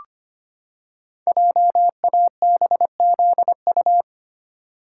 Morse code, heard as a keyed tone, about 25 wpm, 700 Hz. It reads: JABZU
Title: Code Groups - 5 Letters - 25wpm